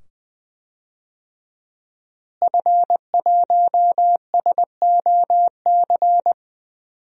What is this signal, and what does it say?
Morse code, heard as a keyed tone, about 20 wpm, 700 Hz. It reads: F1SOC
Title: DXCC-100 Call Signs - 5 Characters - Repeated 2x - 20wpm